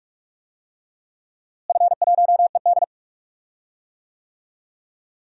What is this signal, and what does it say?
Morse code, heard as a keyed tone, about 43 wpm, 690 Hz. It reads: F1ED
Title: DXCC-100 Call Signs - 4 Characters - Repeated 3x - 45wpm